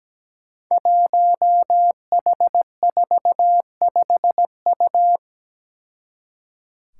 Morse code, heard as a keyed tone, about 17 wpm, 700 Hz. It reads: 1H45U
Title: Five Challenging Letter-Number Combinations 17wpm